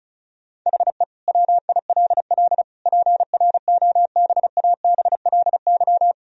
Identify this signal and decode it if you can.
Morse code, heard as a keyed tone, about 36 wpm, 700 Hz. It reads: HE WILL PROBABLY
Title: Sentences from Top 1000 Words - Review Part 4 - 35wpm